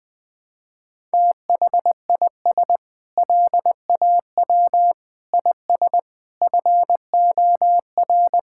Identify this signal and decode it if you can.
Morse code, heard as a keyed tone, about 20 wpm, 700 Hz. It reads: THIS LAW IS FOR
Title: Sentences with 4-Word Phrases - 20wpm